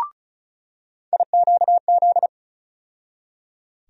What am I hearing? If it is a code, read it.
Morse code, heard as a keyed tone, about 34 wpm, 700 Hz. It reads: IQZ
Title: USA Call Sign Postfixes - 3 Letters 35wpm